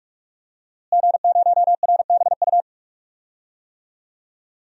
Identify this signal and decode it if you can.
Morse code, heard as a keyed tone, about 46 wpm, 700 Hz. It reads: G0RBU
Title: DXCC-100 Call Signs - 5 Characters - Repeated 3x - 45wpm